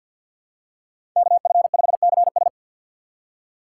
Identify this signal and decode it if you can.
Morse code, heard as a keyed tone, about 47 wpm, 700 Hz. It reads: KF5CS
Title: USA Call Signs - 5 Characters - 4x Character Spacing - 50wpm